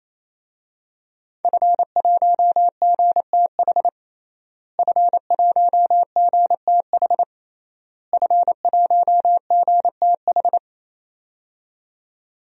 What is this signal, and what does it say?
Morse code, heard as a keyed tone, about 28 wpm, 700 Hz. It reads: F1GTH F1GTH F1GTH
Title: DXCC-100 Call Signs - 5 Characters - Repeated 3x - 28wpm